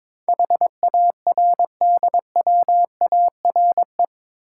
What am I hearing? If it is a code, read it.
Morse code, heard as a keyed tone, about 22 wpm, 700 Hz. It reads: HARDWARE
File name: practice / 22wpm / Eight Letter Words - 22wpm